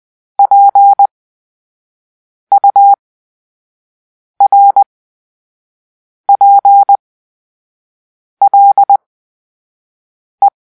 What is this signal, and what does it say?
Morse code, heard as a keyed tone, about 20 wpm, spaced as for 5 wpm, 800 Hz. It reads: PURPLE
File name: Grp-3 Drill-2